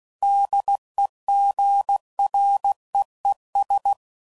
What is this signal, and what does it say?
Morse code, heard as a keyed tone, about 16 wpm, 800 Hz. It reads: DEGREES
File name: CW Code Groups @15WPM / Weather Sets / Weather 3 17